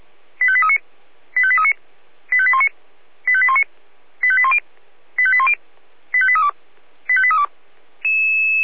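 Five-tone selective call (ZVEI), heard as quick runs of five tones, tones of about 70 ms. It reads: 85839 85839 85819 85819 85810 85810 85832 85832
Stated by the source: ZVEI five-tone selective call received on an SDR.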